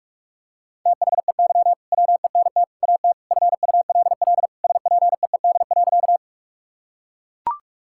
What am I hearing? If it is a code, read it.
Morse code, heard as a keyed tone, about 41 wpm, 700 Hz. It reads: THEY WENT AT FULL SPEED.
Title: Sentences from Top 1000 Words - Review Part 5 - 45wpm